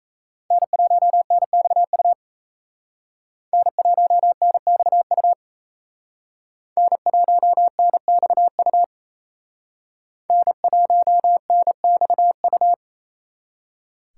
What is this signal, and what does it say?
Morse code, heard as a keyed tone, about 34 wpm, 700 Hz. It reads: N1NXU N1NXU N1NXU N1NXU
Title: Speed Racing - USA 1x3 Call Signs 28wpm